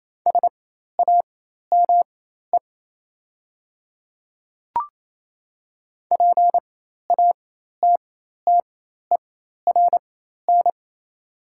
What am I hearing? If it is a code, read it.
Morse code, heard as a keyed tone, about 28 wpm, spaced as for 13 wpm, 690 Hz. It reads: SAME PATTERN
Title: 4x Character Spacing - Top 500 Words - 28wpm